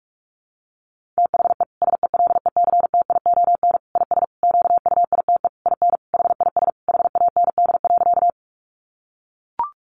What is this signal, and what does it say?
Morse code, heard as a keyed tone, about 44 wpm, 700 Hz. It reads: THE SELECTION IS QUITE IN HIS HAND.
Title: Sentences from Top 1400 Words 45wpm